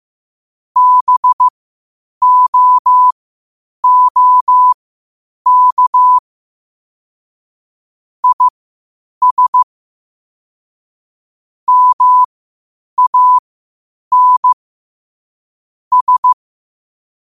Morse code, heard as a keyed tone, about 15 wpm, spaced as for 8 wpm, 1000 Hz. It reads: BOOK IS MAN S
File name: qotdfiles / 8wpm